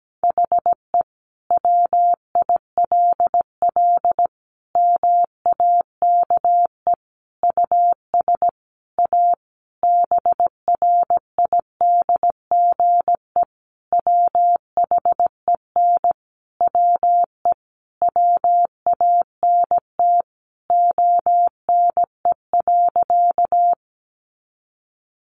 Morse code, heard as a keyed tone, about 17 wpm, 700 Hz. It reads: HE WILL MAKE US A BRIDGE WHEN WE WANT ONE.
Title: Sentences from Top 1200 Words 17wpm